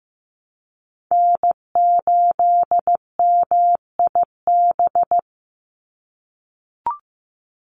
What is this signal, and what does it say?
Morse code, heard as a keyed tone, about 15 wpm, 700 Hz. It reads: N8MIB
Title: USA Call Signs 15wpm